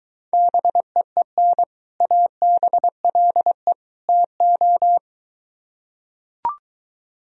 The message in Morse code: BEEN ABLE TO

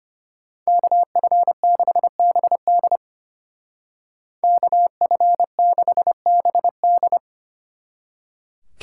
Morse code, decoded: KF6BD KF6BD

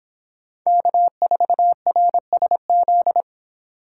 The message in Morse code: K4RSZ

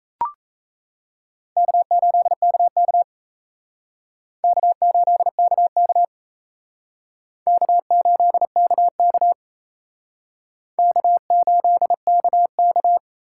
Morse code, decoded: K8KK K8KK K8KK K8KK

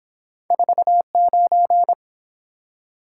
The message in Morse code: 49

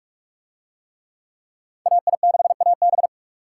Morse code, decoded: AI6AB

